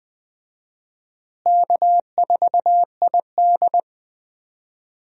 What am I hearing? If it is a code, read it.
Morse code, heard as a keyed tone, about 20 wpm, 700 Hz. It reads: K4ID
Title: USA Call Signs - 4 Characters - Repeated 2x - 20wpm